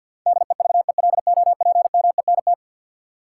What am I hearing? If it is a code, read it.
Morse code, heard as a keyed tone, about 53 wpm, 700 Hz. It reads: DEVELOPMENT
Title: Sets of 3 Words - New Word Drill - 50wpm